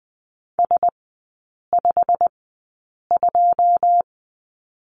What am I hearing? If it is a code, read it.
Morse code, heard as a keyed tone, about 20 wpm, spaced as for 8 wpm, 700 Hz. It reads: S52
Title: 152 Lesson 25e - Letter p - Three Characters 20wpm